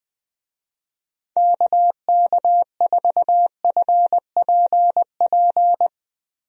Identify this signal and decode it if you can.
Morse code, heard as a keyed tone, about 20 wpm, 700 Hz. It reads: KK4FPP
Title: USA Call Signs - 6 Characters - Repeated 3x - 20wpm